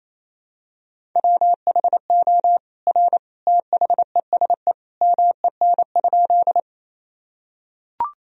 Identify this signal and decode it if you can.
Morse code, heard as a keyed tone, about 28 wpm, 700 Hz. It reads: WHO R THESE MEN?
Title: Sentences from Top 200 Words and Abbreviations 28wpm